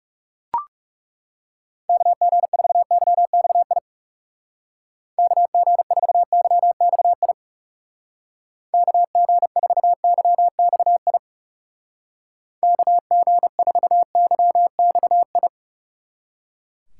Morse code, heard as a keyed tone, about 37 wpm, 700 Hz. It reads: KG4YXI KG4YXI KG4YXI KG4YXI
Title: Speed Racing - USA 2x3 Call Signs 30wpm